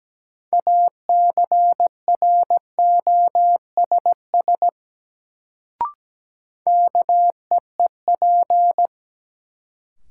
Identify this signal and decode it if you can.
Morse code, heard as a keyed tone, about 17 wpm, 700 Hz. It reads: ACROSS KEEP
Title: Sets of 2 Words - New Word Drill - 17wpm